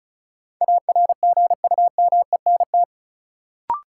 Morse code, heard as a keyed tone, about 34 wpm, 700 Hz. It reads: ARGUMENT
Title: Sentences from Top 1400 Words - New Word Drill Alt - 35wpm